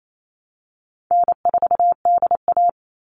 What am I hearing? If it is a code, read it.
Morse code, heard as a keyed tone, about 28 wpm, 700 Hz. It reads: N4DA
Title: NA SOTA Chaser Callsigns 28wpm